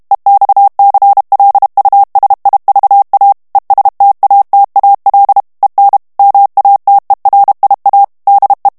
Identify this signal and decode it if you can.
Morse code, heard as a keyed tone, about 32 wpm, 800 Hz. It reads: EXCLUSIVA ESTATAL EN MATERIA DE
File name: ESPAGNOL TEXTE EN CLAIR 2 VITESSE 30 MOTS MINUTES